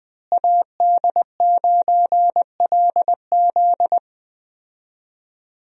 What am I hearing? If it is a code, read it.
Morse code, heard as a keyed tone, about 20 wpm, 700 Hz. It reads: AD9LZ
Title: USA Call Signs - 5 Characters - Repeated 3x - 20wpm